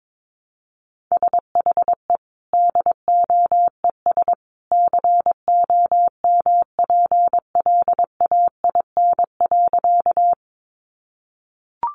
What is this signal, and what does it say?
Morse code, heard as a keyed tone, about 22 wpm, 700 Hz. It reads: SHE DOES COMPLAIN.
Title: Sentences from Top 1300 Words 22wpm